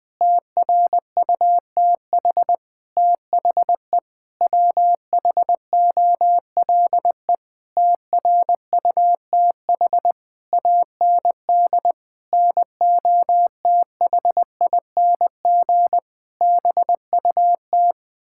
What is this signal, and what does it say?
Morse code, heard as a keyed tone, about 20 wpm, 700 Hz. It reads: TRUTH THE WHOLE TRUTH AND NOTHING BUT
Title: Sentences with 9-Word Phrases - 20wpm